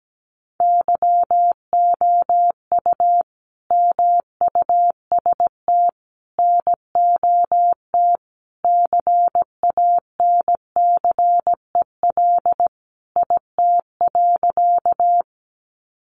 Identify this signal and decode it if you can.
Morse code, heard as a keyed tone, about 17 wpm, 700 Hz. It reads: YOU MUST NOT CANCEL IT.
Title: Sentences from Top 1200 Words 17wpm